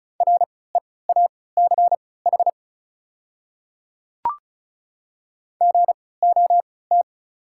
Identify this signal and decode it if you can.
Morse code, heard as a keyed tone, about 35 wpm, spaced as for 20 wpm, 690 Hz. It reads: REACH GOT
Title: 3x Character Spacing - Top 500 Words - 35wpm